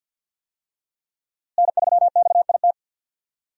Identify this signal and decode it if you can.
Morse code, heard as a keyed tone, about 50 wpm, 700 Hz. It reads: N3XIT
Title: USA Call Signs - 5 Characters - Repeated 3x - 50wpm